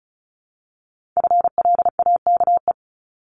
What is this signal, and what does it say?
Morse code, heard as a keyed tone, about 36 wpm, 700 Hz. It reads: FLAKE
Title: Multi-letter Phonemes 35wpm